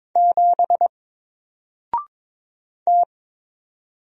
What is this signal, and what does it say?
Morse code, heard as a keyed tone, about 22 wpm, 700 Hz. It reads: 7 T
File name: Single-Letter-Number-22wpm